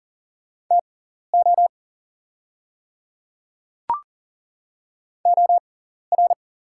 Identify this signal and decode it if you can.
Morse code, heard as a keyed tone, about 40 wpm, spaced as for 14 wpm, 690 Hz. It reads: TO OR
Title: 6x Character Spacing - Top 200 Words - 40wpm